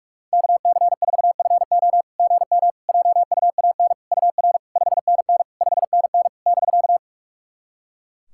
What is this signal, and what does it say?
Morse code, heard as a keyed tone, about 44 wpm, 690 Hz. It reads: KC4FO GM JUAN UR 5NN 5NN <BK>
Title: SOTA QSO Practice 45wpm